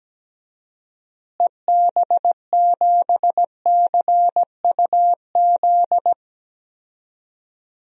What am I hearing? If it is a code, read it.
Morse code, heard as a keyed tone, about 17 wpm, 700 Hz. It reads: EB7CUZ
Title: DXCC-100 Call Signs - 6 Characters - Repeated 2x - 17wpm